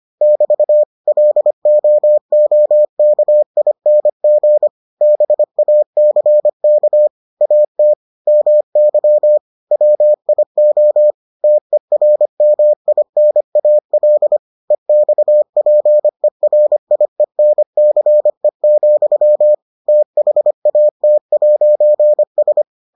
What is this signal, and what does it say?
Morse code, heard as a keyed tone, about 25 wpm, 600 Hz. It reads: <BT> LOOKING BACK AT MY WIO TERMINAL EXPERIENCE, THAT'S